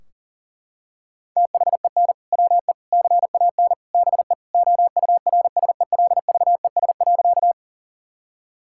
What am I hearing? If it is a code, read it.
Morse code, heard as a keyed tone, about 40 wpm, 700 Hz. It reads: THEN WE CAN BE OURSELVES.
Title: Sentences from Top 1600 Words 40wpm